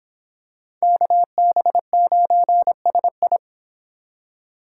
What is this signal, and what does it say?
Morse code, heard as a keyed tone, about 26 wpm, 700 Hz. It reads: KB9SI